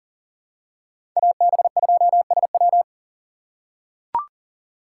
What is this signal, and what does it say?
Morse code, heard as a keyed tone, about 40 wpm, 700 Hz. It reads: AB2SW
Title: USA Call Signs - 6x Character Spacing - 40wpm